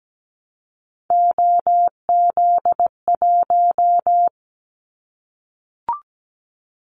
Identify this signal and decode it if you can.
Morse code, heard as a keyed tone, about 17 wpm, 700 Hz. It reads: OZ1
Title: Callsign Prefixes for Top 100 DXCC - 17wpm